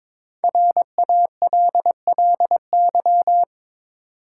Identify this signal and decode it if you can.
Morse code, heard as a keyed tone, about 22 wpm, 690 Hz. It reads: RALLY
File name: 5 Letter Words - First 3 Letters Given - 22wpm